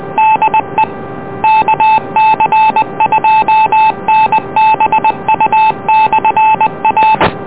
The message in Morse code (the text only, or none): DE KC2NBU/A